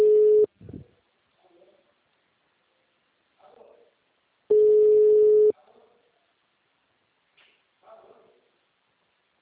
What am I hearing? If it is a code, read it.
Ringback tone (Europe, 425 Hz).